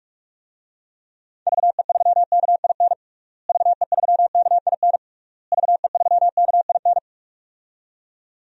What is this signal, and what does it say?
Morse code, heard as a keyed tone, about 46 wpm, 700 Hz. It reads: VE3KIN VE3KIN VE3KIN
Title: DXCC-100 Call Signs - 6 Characters - Repeated 3x - 45wpm